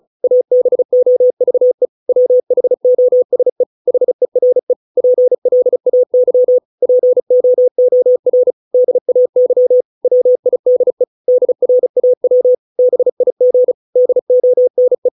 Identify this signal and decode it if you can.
Morse code, heard as a keyed tone, about 34 wpm, 500 Hz. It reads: ABOVE WHOSE HERE PLAY POOR DAY WIDE DRAW BIG DONE